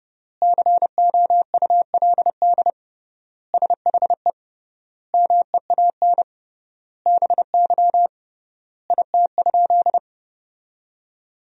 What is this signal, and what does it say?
Morse code, heard as a keyed tone, about 30 wpm, 700 Hz. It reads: COULD SHE MEAN BY IT?
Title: 3x Word-Spacing - Sentences from Top 200 Words - 30wpm